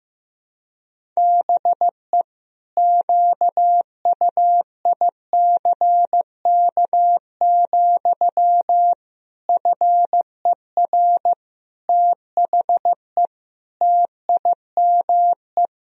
Morse code, heard as a keyed tone, about 15 wpm, 700 Hz. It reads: BE QUICK, FER THE TIME
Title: Sentences from Top 500 Words and Abbreviations 15wpm